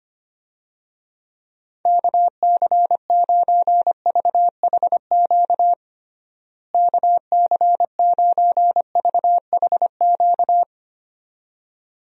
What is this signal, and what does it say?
Morse code, heard as a keyed tone, about 25 wpm, 700 Hz. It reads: KC9VHQ KC9VHQ
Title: USA Call Signs - 6 Characters - Repeated 2x - 25wpm